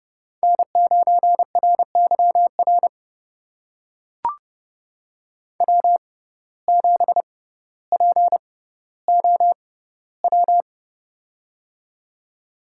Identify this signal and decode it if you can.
Morse code, heard as a keyed tone, about 29 wpm, spaced as for 15 wpm, 700 Hz. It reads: N9RYR W7POW
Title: USA Call Signs - 5 Characters - 6x Character Spacing - 30wpm